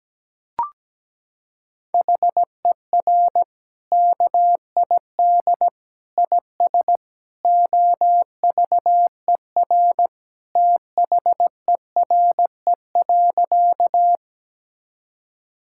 Morse code, heard as a keyed tone, about 17 wpm, 700 Hz. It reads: HER KID IS OVER THERE.
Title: Sentences from Top 1600 Words 17wpm